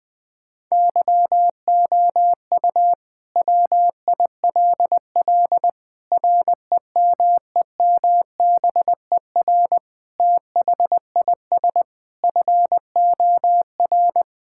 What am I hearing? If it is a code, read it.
Morse code, heard as a keyed tone, about 20 wpm, 700 Hz. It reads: YOU WILL REMEMBER THIS FOR
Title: Sentences with 5-Word Phrases - 20wpm